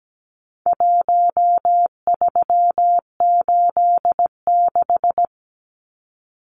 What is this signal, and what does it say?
Morse code, heard as a keyed tone, about 17 wpm, 700 Hz. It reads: 1386
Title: Numbers - Contest Style - 4-Digit 17wpm